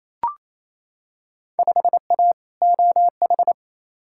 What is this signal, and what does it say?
Morse code, heard as a keyed tone, about 28 wpm, 700 Hz. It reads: 5A OH
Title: Field Day Exchanges Encore - 28wpm